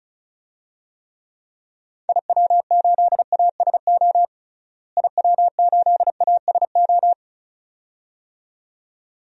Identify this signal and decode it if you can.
Morse code, heard as a keyed tone, about 34 wpm, 700 Hz. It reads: IW8ASO IW8ASO
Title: DXCC-100 Call Signs - 6 Characters - Repeated 2x - 35wpm